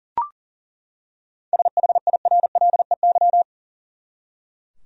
Morse code, heard as a keyed tone, about 37 wpm, 700 Hz. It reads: SHIRLEY
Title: USA Common Names - 40wpm